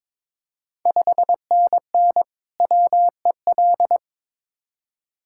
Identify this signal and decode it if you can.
Morse code, heard as a keyed tone, about 21 wpm, 690 Hz. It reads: 5NN WEL
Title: Ontario QSO Party Exchange 22wpm